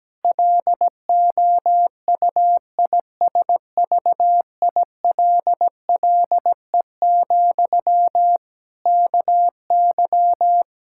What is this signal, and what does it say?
Morse code, heard as a keyed tone, about 17 wpm, 700 Hz. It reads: LOUISVILLE, KY
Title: Practice QSO - Exchange 1 - Beginning - 17wpm